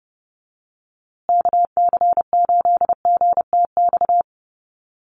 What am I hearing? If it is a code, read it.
Morse code, heard as a keyed tone, about 30 wpm, 700 Hz. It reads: KC8GTX
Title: USA Call Signs Repeated 2x 30wpm